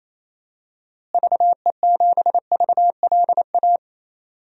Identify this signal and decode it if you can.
Morse code, heard as a keyed tone, about 28 wpm, 700 Hz. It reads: VE7VLA
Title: DXCC-100 Call Signs - 6 Characters - Repeated 2x - 28wpm